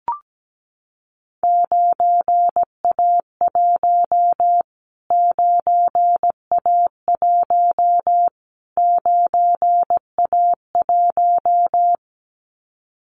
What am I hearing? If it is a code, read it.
Morse code, heard as a keyed tone, about 17 wpm, 700 Hz. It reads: 9A1 9A1 9A1
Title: Callsign Prefixes for Top 25 DXCC - Repeated 3x - 17wpm